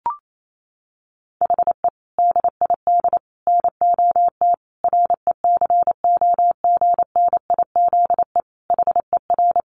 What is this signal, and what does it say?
Morse code, heard as a keyed tone, about 28 wpm, 700 Hz. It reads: HE DID NOT RECOGNIZE HER
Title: Sentences from Top 1400 Words 28wpm